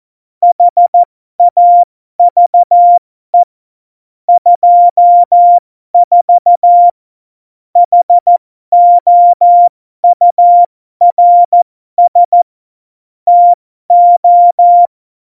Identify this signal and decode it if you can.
Morse code, heard as a keyed tone, about 14 wpm, spaced as for 12 wpm, 700 Hz. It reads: HAVE 24 HOURS TO
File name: Morsekurs / Witze